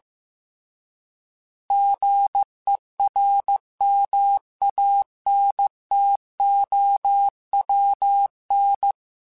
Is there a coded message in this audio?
Yes: Morse code from 1.7 to 8.9 s, heard as a keyed tone, about 15 wpm, 800 Hz. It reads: GERMANTOWN